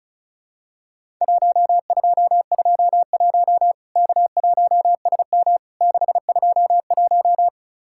Morse code, heard as a keyed tone, about 34 wpm, 700 Hz. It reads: 1221 K1SM 621
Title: Worked All Europe DX Contest - QTC Exchange - 35wpm